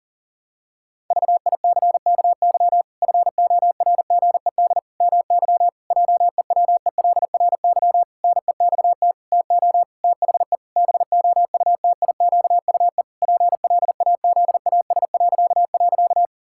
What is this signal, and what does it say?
Morse code, heard as a keyed tone, about 37 wpm, 700 Hz. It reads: VICKY FORGED MY JEWELRY NEXT TO THE BOUTIQUE PLAZAS..
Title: Pangram Challenge v2 - 40wpm